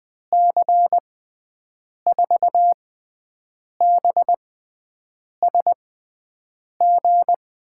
Morse code, heard as a keyed tone, about 20 wpm, spaced as for 7 wpm, 700 Hz. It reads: C4BSG
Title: USA Call Signs - 6 Characters - 6x Character Spacing - 20wpm